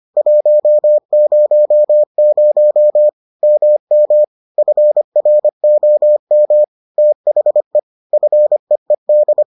Morse code, heard as a keyed tone, about 25 wpm, 600 Hz. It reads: 100 MM FROM THE FEED